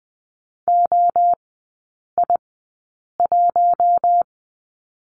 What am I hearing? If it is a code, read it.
Morse code, heard as a keyed tone, about 20 wpm, spaced as for 8 wpm, 700 Hz. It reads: OI1
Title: 212 Lesson 34e - Number 8 - Three Characters 20wpm